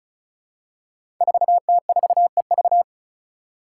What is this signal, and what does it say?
Morse code, heard as a keyed tone, about 34 wpm, 700 Hz. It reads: 4T4EV
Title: Five Challenging Letter-Number Combinations 35wpm